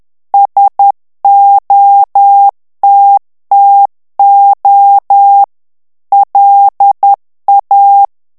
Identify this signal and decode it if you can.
Morse code, heard as a keyed tone, about 10 wpm, 800 Hz. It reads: SOTTO LA
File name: morse mp3 / ITALIEN texte clair 1 vitesse 10 mots minute